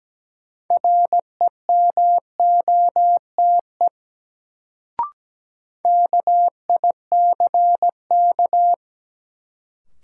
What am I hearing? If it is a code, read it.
Morse code, heard as a keyed tone, about 17 wpm, 700 Hz. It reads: REMOTE KICK
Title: Sentences from Top 1300 Words - New Word Drill Alt - 17wpm